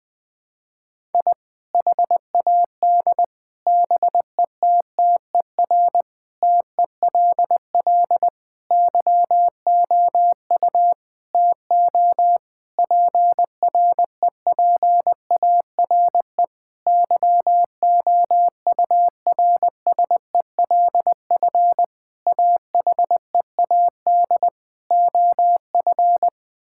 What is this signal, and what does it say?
Morse code, heard as a keyed tone, about 20 wpm, 700 Hz. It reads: I HAD BETTER TELL YOU TO PREPARE YOURSELF AHEAD OF
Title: Sentences from Top 1000 Words - Review Part 6 - 20wpm